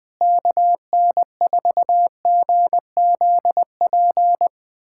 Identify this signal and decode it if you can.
Morse code, heard as a keyed tone, about 20 wpm, 700 Hz. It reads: KN4GZP
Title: USA Callsign DE Callsign - 6 Characters - 20wpm